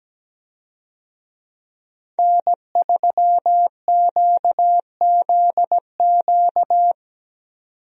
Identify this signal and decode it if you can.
Morse code, heard as a keyed tone, about 17 wpm, 700 Hz. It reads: N3QZQ
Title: USA Call Signs - 5 Characters - 17wpm